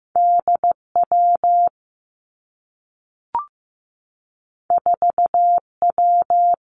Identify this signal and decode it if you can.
Morse code, heard as a keyed tone, about 15 wpm, 700 Hz. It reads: D W 4 W
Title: Two Challenging Letter-Number Combinations 15wpm